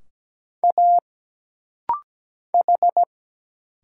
Morse code, heard as a keyed tone, about 17 wpm, 700 Hz. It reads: A H
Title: Single Letter-Number - Rapid-Fire 17wpm